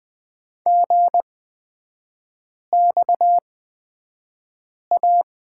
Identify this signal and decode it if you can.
Morse code, heard as a keyed tone, about 20 wpm, spaced as for 5 wpm, 700 Hz. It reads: GXA
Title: 243 Lesson 39e - Letter z - Three Characters 20wpm